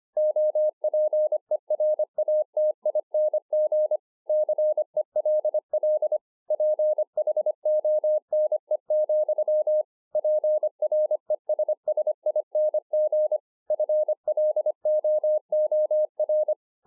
Morse code, heard as a keyed tone, about 25 wpm, 610 Hz. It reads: OPERATING CELL PHONE, PRESSING FLOOR